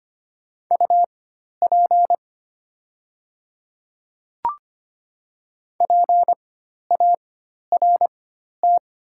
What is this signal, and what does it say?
Morse code, heard as a keyed tone, about 25 wpm, spaced as for 12 wpm, 700 Hz. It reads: UP PART
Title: 4x Character Spacing - Top 200 Words - 25wpm